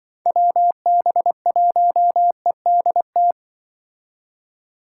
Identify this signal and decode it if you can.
Morse code, heard as a keyed tone, about 24 wpm, 700 Hz. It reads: WB1EDT
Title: Speed Racing - USA 2x3 Call Signs 20wpm